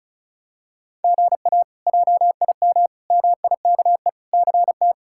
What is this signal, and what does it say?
Morse code, heard as a keyed tone, about 36 wpm, 700 Hz. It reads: GA JIM MIKE CT
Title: K1USN SST QSO Practice - 35wpm